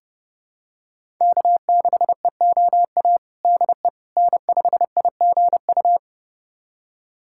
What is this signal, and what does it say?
Morse code, heard as a keyed tone, about 30 wpm, 700 Hz. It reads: K6EOA DE N5IGU
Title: USA Callsign DE Callsign - 5 Characters - 30wpm